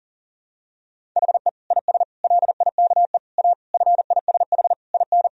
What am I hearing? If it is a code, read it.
Morse code, heard as a keyed tone, about 40 wpm, 700 Hz. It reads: HE IS LIKE A FISH IN
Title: Sentences from Top 1000 Words - Review Part 1 - 40wpm